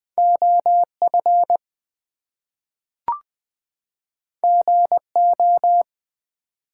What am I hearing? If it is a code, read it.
Morse code, heard as a keyed tone, about 20 wpm, 700 Hz. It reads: OF GO
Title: Two Letter Words - 20wpm